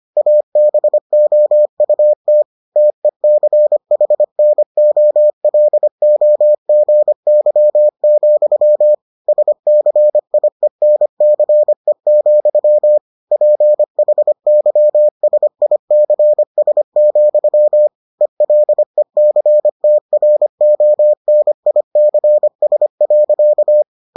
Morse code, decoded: ABOUT TECHNOLOGY, SCIENCE, PHYSICS, ELECTRONICS.

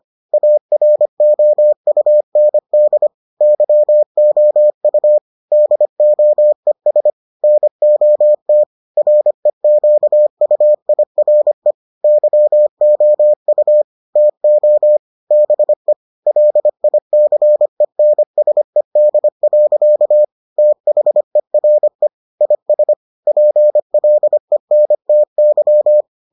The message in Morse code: AROUND YOU DOES NOT REQUIRE YOU TO BE LICENSED. THERE IS PLENTY

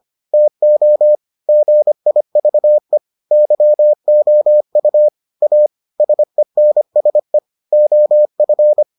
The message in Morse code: TO GIVE YOU A SENSE OF

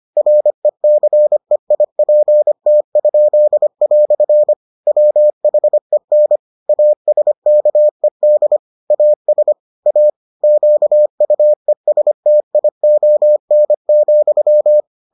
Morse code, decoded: RECEIPT?" WHEN ASKED AS A QUESTION,

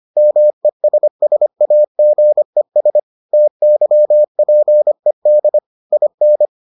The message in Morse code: MESSAGES TYPED IN

